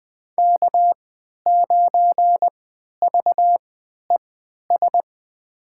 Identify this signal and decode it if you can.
Morse code, heard as a keyed tone, about 21 wpm, spaced as for 12 wpm, 700 Hz. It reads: K9VES
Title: USA Call Signs - 5 Characters - 3x Character Spacing - 20wpm